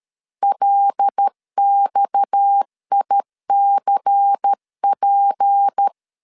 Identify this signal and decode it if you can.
Morse code, heard as a keyed tone, about 13 wpm, 800 Hz. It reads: LXICP
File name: CW / DET / 1 6 1500C S A 12wpm 3440